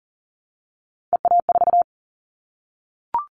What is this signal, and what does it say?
Morse code, heard as a keyed tone, about 40 wpm, 700 Hz. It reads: EA4
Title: Callsign Prefixes for Top 25 DXCC - Repeated 3x - 40wpm